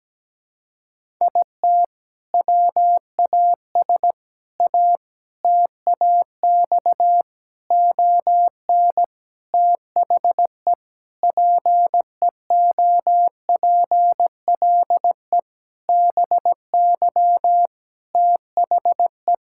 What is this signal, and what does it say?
Morse code, heard as a keyed tone, about 17 wpm, 700 Hz. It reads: IT WAS A TAX ON THE PEOPLE BY THE
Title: Sentences from Top 1000 Words - Review Part 5 - 17wpm